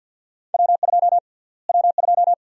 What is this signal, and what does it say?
Morse code, heard as a keyed tone, about 46 wpm, 700 Hz. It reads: W2 W2
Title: USA Call Sign Prefixes - Repeated 2x - 50wpm